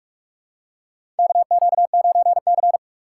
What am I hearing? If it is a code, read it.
Morse code, heard as a keyed tone, about 46 wpm, 700 Hz. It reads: KQ9C
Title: USA Call Signs - 4 Characters - 4x Character Spacing - 45wpm